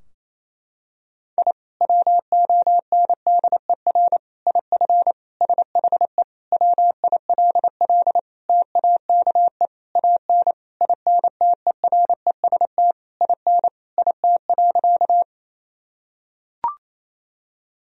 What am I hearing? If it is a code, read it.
Morse code, heard as a keyed tone, about 27 wpm, 700 Hz. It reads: I WONDER IF SHE WILL TAKE AN INTEREST IN IT.
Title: Sentences from Top 1000 Words - Review Part 3 - 28wpm